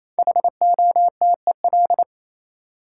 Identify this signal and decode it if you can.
Morse code, heard as a keyed tone, about 28 wpm, 700 Hz. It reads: HOTEL
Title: Sentences from Top 1600 Words - New Word Drill - 28wpm